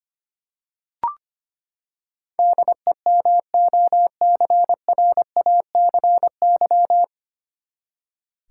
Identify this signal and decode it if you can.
Morse code, heard as a keyed tone, about 25 wpm, 700 Hz. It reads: DEMOCRACY
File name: Nine Letter Words - 25wpm